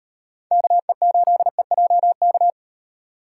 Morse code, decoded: KE8EJK

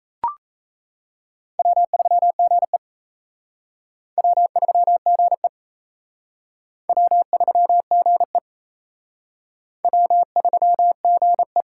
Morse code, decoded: W3GE W3GE W3GE W3GE

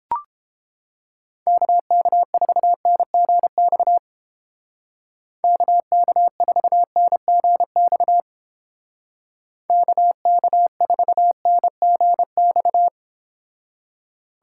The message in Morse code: KK4NGX KK4NGX KK4NGX

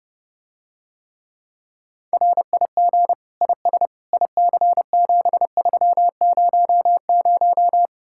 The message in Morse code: RIG IS IC7300

